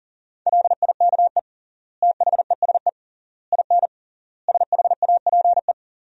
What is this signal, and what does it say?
Morse code, heard as a keyed tone, about 37 wpm, 700 Hz. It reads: LIKE THESE IN SHAPE
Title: 3x Word-Spacing - Sentences from Top 500 Words - 40wpm